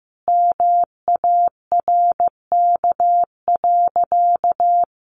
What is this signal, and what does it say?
Morse code, heard as a keyed tone, about 15 wpm, 700 Hz. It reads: MARK.
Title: Sentences from Top 1100 Words 15wpm